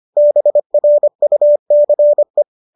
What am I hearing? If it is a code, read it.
Morse code, heard as a keyed tone, about 25 wpm, 590 Hz. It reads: BRUCE
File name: qso201 25